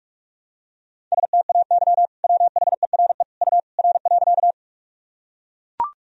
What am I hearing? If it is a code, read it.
Morse code, heard as a keyed tone, about 41 wpm, 700 Hz. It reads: STAY WHERE U R.
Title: Sentences from Top 500 Words and Abbreviations 45wpm